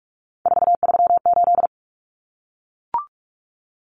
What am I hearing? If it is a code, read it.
Morse code, heard as a keyed tone, about 44 wpm, 700 Hz. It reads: 438
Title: Common Cut Numbers - 3-Digit 45wpm